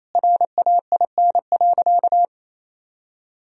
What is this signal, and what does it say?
Morse code, heard as a keyed tone, about 28 wpm, 700 Hz. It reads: RAIN.
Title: Sentences from Top 1000 Words - Review Part 3 - 28wpm